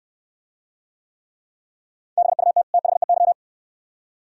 Morse code, heard as a keyed tone, about 64 wpm, 700 Hz. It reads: BUT THEY